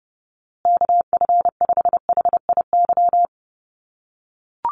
Morse code, heard as a keyed tone, about 30 wpm, 700 Hz. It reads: KF5HIY
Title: USA Call Signs Repeated 3x 30wpm